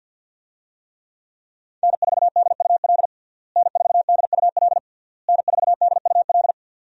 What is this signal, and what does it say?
Morse code, heard as a keyed tone, about 46 wpm, 700 Hz. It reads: N4DUL N4DUL N4DUL
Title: USA Call Signs - 5 Characters - Repeated 3x - 50wpm